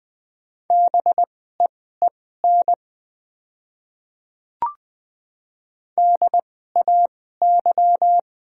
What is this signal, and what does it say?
Morse code, heard as a keyed tone, about 20 wpm, spaced as for 15 wpm, 690 Hz. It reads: BEEN DAY
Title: 2x Character Spacing - Top 100 Words - 20wpm